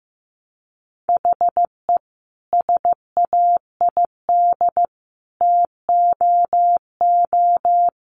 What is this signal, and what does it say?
Morse code, heard as a keyed tone, about 15 wpm, 700 Hz. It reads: HE SAID TOO
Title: Sentences from Top 200 Words 15wpm